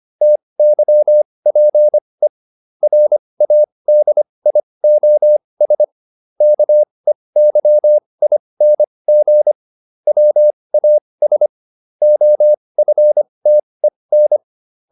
Morse code, heard as a keyed tone, about 25 wpm, spaced as for 20 wpm, 600 Hz. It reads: TYPE RADIOS KEYING WAS OFTEN